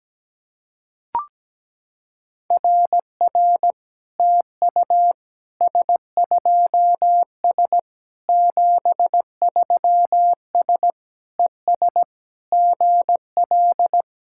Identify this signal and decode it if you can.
Morse code, heard as a keyed tone, about 17 wpm, 700 Hz. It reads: RR TU S2S 73S ES GL
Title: SOTA QSO Practice 17wpm